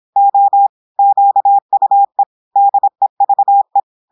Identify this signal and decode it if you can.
Morse code, heard as a keyed tone, about 26 wpm, 800 Hz. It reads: O QUE DEVE
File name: portugais texte en clair 2 vitesse 26 mots minutes